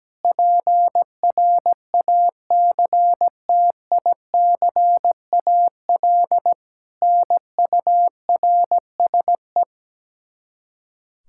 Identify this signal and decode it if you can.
Morse code, heard as a keyed tone, about 17 wpm, 700 Hz. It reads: PRACTICAL NURSE
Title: Practice QSO - Exchange 3 - More Info - 17wpm